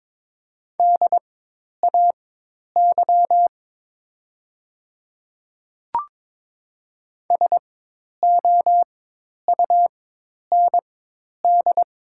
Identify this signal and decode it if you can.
Morse code, heard as a keyed tone, about 22 wpm, spaced as for 10 wpm, 700 Hz. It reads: DAY SOUND
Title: 4x Character Spacing - Top 100 Words - 22wpm